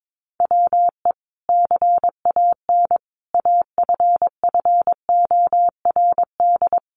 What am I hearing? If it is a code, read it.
Morse code, heard as a keyed tone, about 22 wpm, 700 Hz. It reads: WE CAN AFFORD
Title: Sentences from Top 1100 Words 22wpm